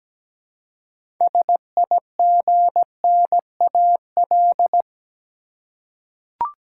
Morse code, heard as a keyed tone, about 17 wpm, 700 Hz. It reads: SIGNAL
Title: Six Letter Words - 17wpm